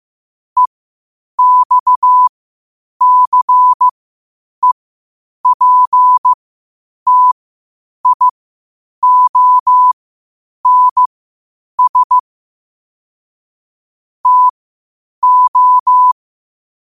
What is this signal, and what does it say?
Morse code, heard as a keyed tone, about 15 wpm, spaced as for 8 wpm, 1000 Hz. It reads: EXCEPTIONS TO
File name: qotdfiles / 8wpm